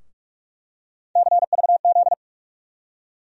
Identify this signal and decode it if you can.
Morse code, heard as a keyed tone, about 46 wpm, 700 Hz. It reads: CVZ
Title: USA Call Sign Postfixes - 3 Letters 45wpm